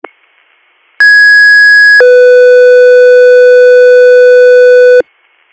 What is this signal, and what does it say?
Two-tone page, heard as a one-second tone, then a three-second tone, about 1650 then 500 Hz. It is heard from 1.0 to 5.0 s.